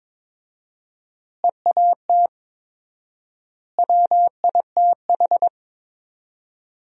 Morse code, heard as a keyed tone, about 21 wpm, 700 Hz. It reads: EAT WITH
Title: 4x Word-Spacing - Sentences from Top 300 Words - 22wpm